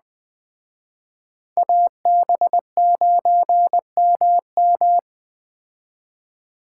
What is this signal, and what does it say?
Morse code, heard as a keyed tone, about 20 wpm, 700 Hz. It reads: AB9MM
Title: Speed Racing - USA Call Signs 17wpm